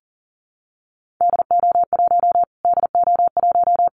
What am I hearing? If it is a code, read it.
Morse code, heard as a keyed tone, about 40 wpm, 700 Hz. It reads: DO1 DO1
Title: Callsign Prefixes for Top 25 DXCC - Repeated 3x - 40wpm